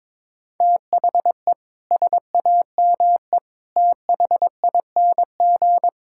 Morse code, heard as a keyed tone, about 22 wpm, 700 Hz. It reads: THE SAME THING
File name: practice / 22wpm / Sets of 7 Words - 22wpm